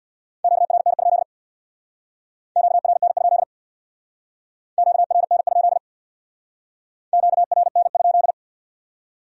Morse code, heard as a keyed tone, about 59 wpm, 700 Hz. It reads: QRN? QRN? QRN? QRN?
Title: Speed-Racing Q-Codes - 50wpm